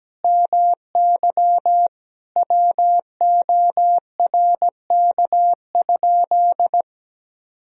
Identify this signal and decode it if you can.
Morse code, heard as a keyed tone, about 17 wpm, 700 Hz. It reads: MY WORK?
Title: Sentences from Top 500 Words and Abbreviations 17wpm